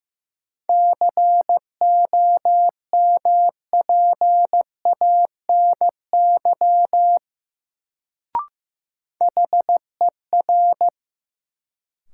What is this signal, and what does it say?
Morse code, heard as a keyed tone, about 15 wpm, 700 Hz. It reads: COMPANY HER
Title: Sets of 2 Words - New Word Drill - 15wpm